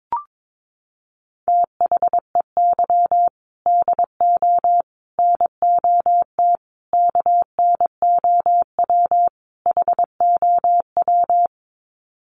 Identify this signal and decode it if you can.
Morse code, heard as a keyed tone, about 22 wpm, 700 Hz. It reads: THEY DO NOT KNOW HOW
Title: Sets of 5 Words Encore2 22wpm